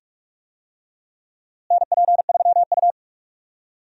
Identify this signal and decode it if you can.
Morse code, heard as a keyed tone, about 46 wpm, 700 Hz. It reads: NP3U
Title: USA Call Signs - 4 Characters - Repeated 2x - 45wpm